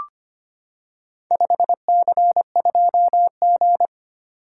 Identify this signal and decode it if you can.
Morse code, heard as a keyed tone, about 25 wpm, 700 Hz. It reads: 5C2G
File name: practice / 25wpm / Four Letter-Number Combinations - 25wpm